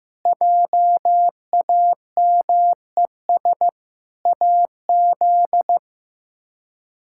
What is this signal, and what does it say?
Morse code, heard as a keyed tone, about 15 wpm, 700 Hz. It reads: JAMES AZ
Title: K1USN SST QSO Practice - 15wpm